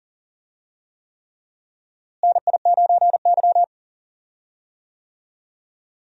Morse code, heard as a keyed tone, about 39 wpm, 700 Hz. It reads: NI9Y
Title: USA Call Signs - 4 Characters - Repeated 3x - 40wpm